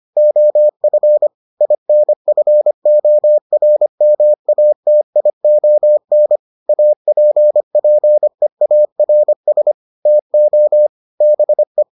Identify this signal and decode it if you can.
Morse code, heard as a keyed tone, about 25 wpm, 600 Hz. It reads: OF INFORMATION APPEARS TO BE